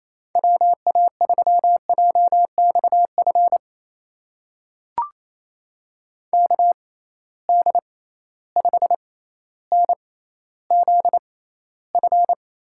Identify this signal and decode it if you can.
Morse code, heard as a keyed tone, about 28 wpm, spaced as for 14 wpm, 700 Hz. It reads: WA3JXF KD5NZF
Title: USA Call Signs - 6 Characters - 6x Character Spacing - 28wpm